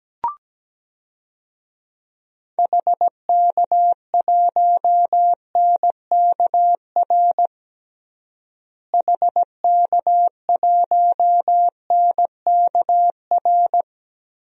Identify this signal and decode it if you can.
Morse code, heard as a keyed tone, about 17 wpm, 700 Hz. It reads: HK1NKR HK1NKR
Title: DXCC-100 Call Signs - 6 Characters - Repeated 2x - 17wpm